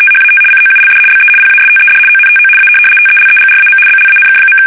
RTTY, about 75 baud. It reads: THAT COULD BE PASTED ONTO A TELEGRAM BLANK.
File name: fsk850-75B / teletype-corp